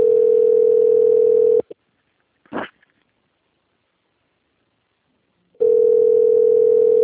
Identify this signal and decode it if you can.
Ringback tone (North America), 440 + 480 Hz.